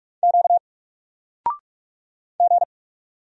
Morse code, heard as a keyed tone, about 46 wpm, 700 Hz. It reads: Q G
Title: Single Letter-Number 45wpm